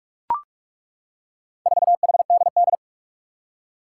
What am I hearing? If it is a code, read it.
Morse code, heard as a keyed tone, about 41 wpm, 700 Hz. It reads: 4HDD
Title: Four Challenging Letter-Number Combinations 45wpm